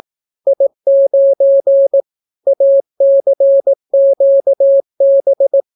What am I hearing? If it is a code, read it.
Morse code, heard as a keyed tone, about 18 wpm, 550 Hz. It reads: I9 ACQB